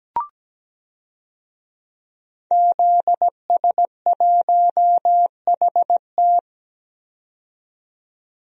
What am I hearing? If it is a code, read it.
Morse code, heard as a keyed tone, about 17 wpm, 700 Hz. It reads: ZS1HT
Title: DXCC-100 Call Signs - 5 Characters - 17wpm